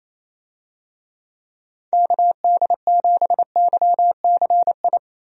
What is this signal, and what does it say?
Morse code, heard as a keyed tone, about 28 wpm, 700 Hz. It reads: KD7YCI
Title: USA Call Signs - 6 Characters - Repeated 2x - 28wpm